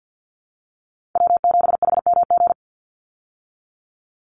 Morse code, heard as a keyed tone, about 51 wpm, 700 Hz. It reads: W7HMG
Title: USA Call Signs 50wpm